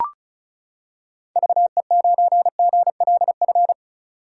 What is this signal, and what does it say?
Morse code, heard as a keyed tone, about 36 wpm, 700 Hz. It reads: VE9GLF
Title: K1USN SST QSO Practice - 35wpm